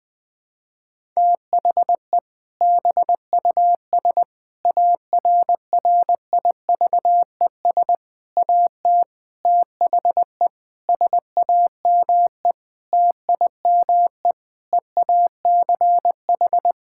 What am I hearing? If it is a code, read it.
Morse code, heard as a keyed tone, about 20 wpm, 700 Hz. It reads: THE BUS ARRIVES AT THE SAME TIME EACH
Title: Difficult Sentences with Hints - 20wpm